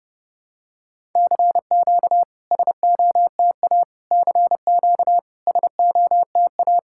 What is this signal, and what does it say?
Morse code, heard as a keyed tone, about 30 wpm, 700 Hz. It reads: CQ SOTA CQ SOTA
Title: SOTA QSO Practice 30wpm